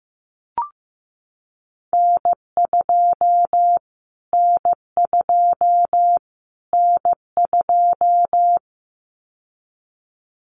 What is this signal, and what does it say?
Morse code, heard as a keyed tone, about 15 wpm, 700 Hz. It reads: N2 N2 N2
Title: Callsign Prefixes for Top 25 DXCC - Repeated 3x - 15wpm